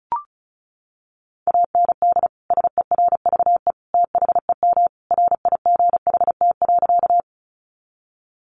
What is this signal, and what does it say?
Morse code, heard as a keyed tone, about 34 wpm, 700 Hz. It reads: AND SERVE THEM RIGHT.
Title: Sentences from Top 400 Words 35wpm